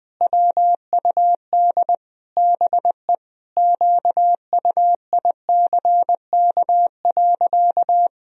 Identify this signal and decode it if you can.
Morse code, heard as a keyed tone, about 20 wpm, 700 Hz. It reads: WUD BE QUICK.
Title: Sentences from Top 500 Words and Abbreviations 20wpm